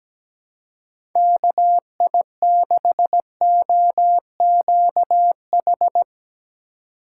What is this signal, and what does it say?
Morse code, heard as a keyed tone, about 17 wpm, 700 Hz. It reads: KI6OQH
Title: USA Call Signs - 6 Characters - 4x Character Spacing - 17wpm